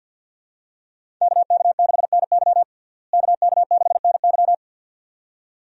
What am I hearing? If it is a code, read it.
Morse code, heard as a keyed tone, about 50 wpm, 700 Hz. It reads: KK6NY KK6NY
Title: USA Call Signs - 5 Characters - Repeated 2x - 50wpm